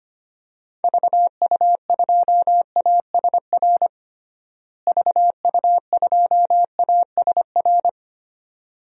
Morse code, heard as a keyed tone, about 25 wpm, 700 Hz. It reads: VU2ASR VU2ASR
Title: DXCC-100 Call Signs - 6 Characters - Repeated 2x - 25wpm